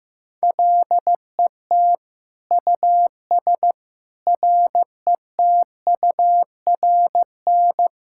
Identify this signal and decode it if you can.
Morse code, heard as a keyed tone, about 15 wpm, 700 Hz. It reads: LET US RETURN
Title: Sentences from Top 1000 Words - Review Part 6 - 15wpm